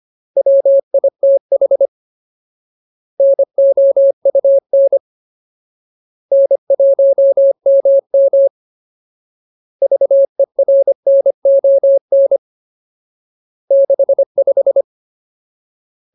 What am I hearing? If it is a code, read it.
Morse code, heard as a keyed tone, about 25 wpm, 550 Hz. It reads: WITH NOUN N1MM VERNON 65
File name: Ultimate Word List / 25WPM Ultimate